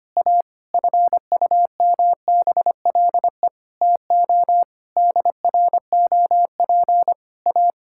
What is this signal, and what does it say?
Morse code, heard as a keyed tone, about 25 wpm, 700 Hz. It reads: A FUMBLE TO DROP A
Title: Pangram Challenge v2 - 25wpm